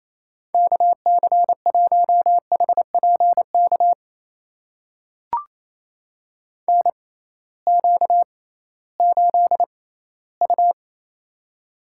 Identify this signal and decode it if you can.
Morse code, heard as a keyed tone, about 28 wpm, spaced as for 16 wpm, 700 Hz. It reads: KC1HPK NQ8U
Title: USA Call Signs - 6x Character Spacing - 28wpm